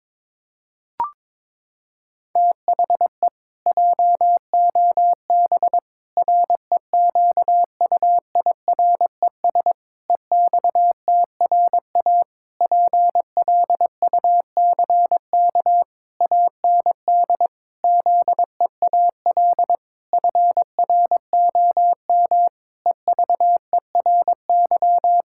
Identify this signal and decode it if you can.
Morse code, heard as a keyed tone, about 21 wpm, 700 Hz. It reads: THE JOB REQUIRES EXTRA PLUCK AND ZEAL FROM EVERY
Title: Pangram Challenge v2 - 22wpm